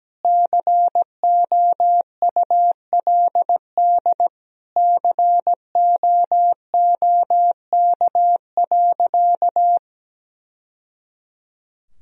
Morse code, heard as a keyed tone, about 17 wpm, 700 Hz. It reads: COULD COOK.
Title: Sentences from Top 1000 Words - Review Part 4 - 17wpm